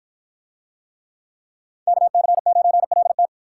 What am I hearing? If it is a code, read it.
Morse code, heard as a keyed tone, about 53 wpm, 700 Hz. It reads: KC9LT